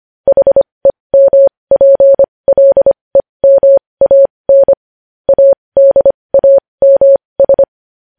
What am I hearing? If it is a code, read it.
Morse code, heard as a keyed tone, about 25 wpm, spaced as for 20 wpm, 550 Hz. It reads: HEMPLEMAN ADAMS
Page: news headlines in morse code at 20 wpm